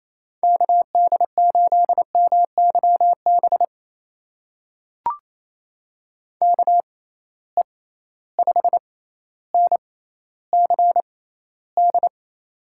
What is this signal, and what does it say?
Morse code, heard as a keyed tone, about 28 wpm, spaced as for 14 wpm, 700 Hz. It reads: KD8MYB KE5NCD
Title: USA Call Signs - 6 Characters - 6x Character Spacing - 28wpm